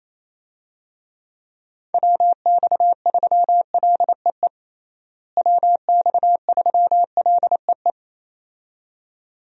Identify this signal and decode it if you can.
Morse code, heard as a keyed tone, about 28 wpm, 700 Hz. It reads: WX3LEE WX3LEE
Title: USA Call Signs - 6 Characters - Repeated 2x - 28wpm